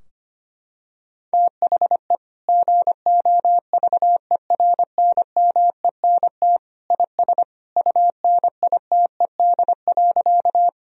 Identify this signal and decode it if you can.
Morse code, heard as a keyed tone, about 25 wpm, 700 Hz. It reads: THE GOVERNMENT IS UNITED.
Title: Sentences from Top 1000 Words - Review Part 4 - 25wpm